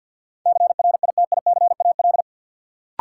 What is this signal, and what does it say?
Morse code, heard as a keyed tone, about 53 wpm, 700 Hz. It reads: CRITICAL
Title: Sentences from Top 1100 Words - New Word Drill Alt - 50wpm